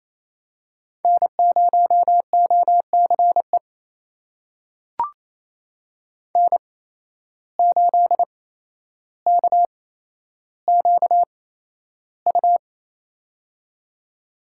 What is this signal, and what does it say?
Morse code, heard as a keyed tone, about 28 wpm, spaced as for 12 wpm, 700 Hz. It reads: N0OCE N8KQU
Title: USA Call Signs - 8x Character Spacing - 28wpm